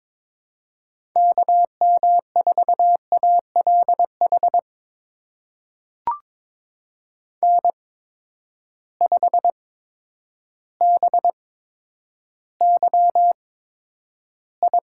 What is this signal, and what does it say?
Morse code, heard as a keyed tone, about 22 wpm, spaced as for 10 wpm, 700 Hz. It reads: KM4ALH N5BYI
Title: USA Call Signs - 8x Character Spacing - 22wpm